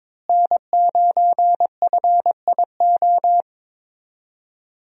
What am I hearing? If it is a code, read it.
Morse code, heard as a keyed tone, about 21 wpm, 700 Hz. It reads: N9FIO
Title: USA Call Signs - 5 Characters - Repeated 3x - 22wpm